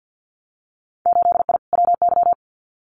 Morse code, heard as a keyed tone, about 50 wpm, 700 Hz. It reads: 8I WY
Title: Winter Field Day Exchanges Encore 50wpm